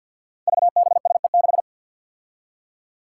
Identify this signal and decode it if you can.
Morse code, heard as a keyed tone, about 50 wpm, 700 Hz. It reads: VBSE6
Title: Five Challenging Letter-Number Combinations 50wpm